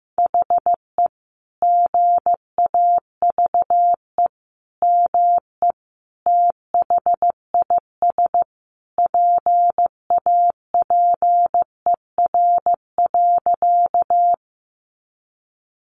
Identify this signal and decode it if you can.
Morse code, heard as a keyed tone, about 15 wpm, 700 Hz. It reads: HE GAVE ME THIS PAPER.
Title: Sentences from Top 500 Words 15wpm